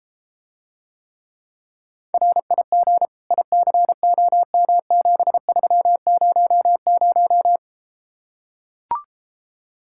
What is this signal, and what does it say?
Morse code, heard as a keyed tone, about 33 wpm, 700 Hz. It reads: RIG ICOM7300